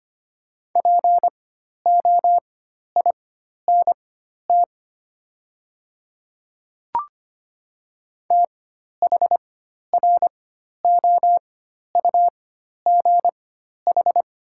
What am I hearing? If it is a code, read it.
Morse code, heard as a keyed tone, about 25 wpm, spaced as for 12 wpm, 700 Hz. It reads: POINT THROUGH
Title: 4x Character Spacing - Top 300 Words - 25wpm